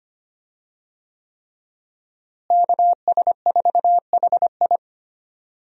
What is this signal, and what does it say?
Morse code, heard as a keyed tone, about 25 wpm, 700 Hz. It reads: KS4HI
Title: USA Call Signs - 5 Characters - Repeated 3x - 25wpm